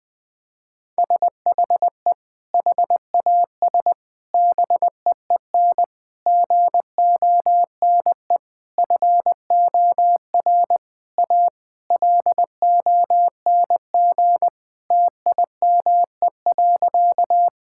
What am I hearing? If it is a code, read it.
Morse code, heard as a keyed tone, about 20 wpm, 700 Hz. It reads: SHE HAS BEEN GONE FOR A LONG TIME.
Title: Sentences with 3-Word Phrases - 20wpm